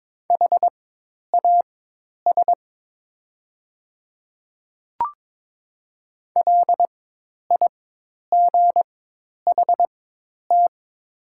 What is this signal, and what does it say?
Morse code, heard as a keyed tone, about 22 wpm, spaced as for 10 wpm, 700 Hz. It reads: HAS LIGHT
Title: 4x Character Spacing - Top 200 Words - 22wpm